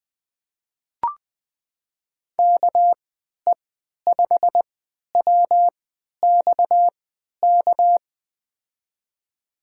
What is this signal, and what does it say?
Morse code, heard as a keyed tone, about 20 wpm, spaced as for 11 wpm, 700 Hz. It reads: KE5WXK
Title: USA Call Signs - 3x Character Spacing - 20wpm